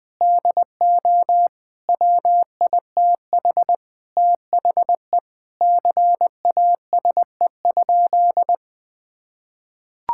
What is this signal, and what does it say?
Morse code, heard as a keyed tone, about 20 wpm, 700 Hz. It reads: DO WITH THE CASE?
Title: Sentences with 4-Word Phrases - 20wpm